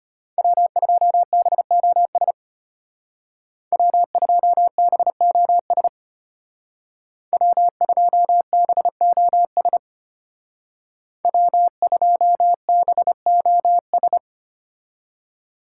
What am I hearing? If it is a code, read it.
Morse code, heard as a keyed tone, about 32 wpm, 700 Hz. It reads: W2BOS W2BOS W2BOS W2BOS